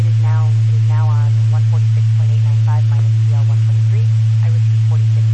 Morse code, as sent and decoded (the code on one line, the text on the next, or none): none